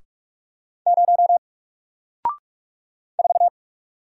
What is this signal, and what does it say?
Morse code, heard as a keyed tone, about 48 wpm, 700 Hz. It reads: 0 4
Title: Single Letter-Number Alt 45wpm